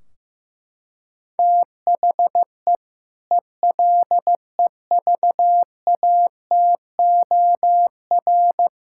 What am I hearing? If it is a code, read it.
Morse code, heard as a keyed tone, about 16 wpm, 700 Hz. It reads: THE ELEVATOR
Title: Sentences from Top 1600 Words 15wpm